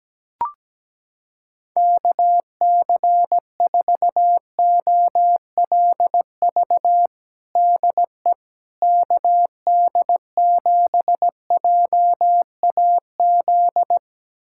Morse code, heard as a keyed tone, about 17 wpm, 700 Hz. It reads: KC4OLV DE KD7JAZ
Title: USA Callsign DE Callsign - 6 Characters - 17wpm